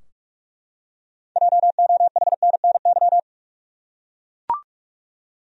Morse code, heard as a keyed tone, about 44 wpm, 700 Hz. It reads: JOHNNY
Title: USA Common Names - 45wpm